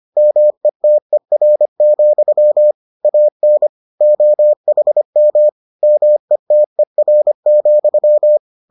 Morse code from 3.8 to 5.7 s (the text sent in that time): OHM